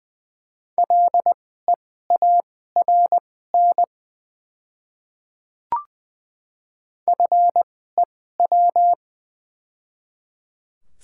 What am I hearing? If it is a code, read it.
Morse code, heard as a keyed tone, about 20 wpm, spaced as for 14 wpm, 690 Hz. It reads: LEARN FEW
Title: 2x Character Spacing - Top 300 Words - 20wpm